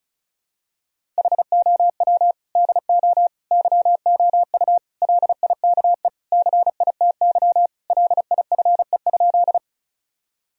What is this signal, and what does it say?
Morse code, heard as a keyed tone, about 36 wpm, 700 Hz. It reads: HOW DO YOU LIKE CITY LIFE?
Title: Sentences from Top 1000 Words - Review Part 1 - 35wpm